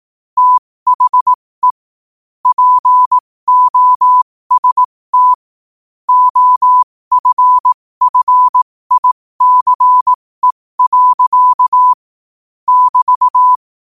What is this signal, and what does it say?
Morse code, heard as a keyed tone, about 18 wpm, spaced as for 15 wpm, 1000 Hz. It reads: THE POST OFFICE. <BT>
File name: qotdfiles / 15wpm